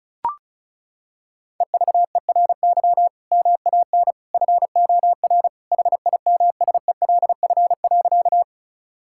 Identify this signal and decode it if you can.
Morse code, heard as a keyed tone, about 36 wpm, 700 Hz. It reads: EVERY MAN FOR HIMSELF.
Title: Sentences from Top 1600 Words 35wpm